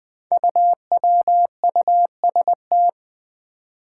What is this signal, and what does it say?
Morse code, heard as a keyed tone, about 20 wpm, 700 Hz. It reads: UWUST
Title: Five Challenging Letter-Number Combinations 20wpm